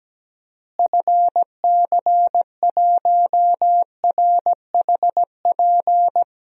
Morse code, heard as a keyed tone, about 17 wpm, 700 Hz. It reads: FC1RHP
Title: DXCC-100 Call Signs - 6 Characters - Repeated 3x - 17wpm